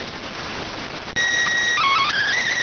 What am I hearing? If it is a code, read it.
SSTV (slow-scan television), PD 290.